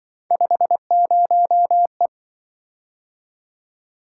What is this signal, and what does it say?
Morse code, heard as a keyed tone, about 25 wpm, 700 Hz. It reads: 50E